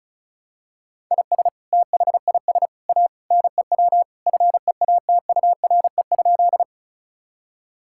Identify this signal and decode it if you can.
Morse code, heard as a keyed tone, about 36 wpm, 700 Hz. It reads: IS THIS A NEW FEATURE?
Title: Sentences from Top 1000 Words - Review Part 5 - 35wpm